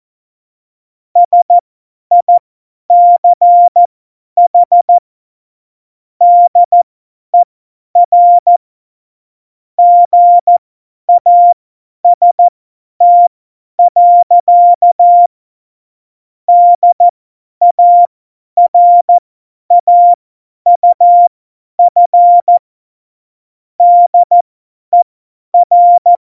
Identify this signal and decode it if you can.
Morse code, heard as a keyed tone, about 14 wpm, spaced as for 10 wpm, 700 Hz. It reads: SICH DER GAST. DARAUF DER